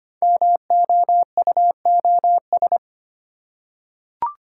Morse code, heard as a keyed tone, about 25 wpm, 700 Hz. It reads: MOUOS
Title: Code Groups - 5 Letters - 25wpm